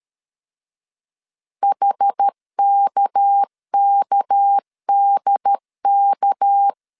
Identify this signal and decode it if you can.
Morse code, heard as a keyed tone, about 13 wpm, 800 Hz. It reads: HKKDK
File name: LEZIONI / 6 PKX0 / 6 PK 12wpm 2006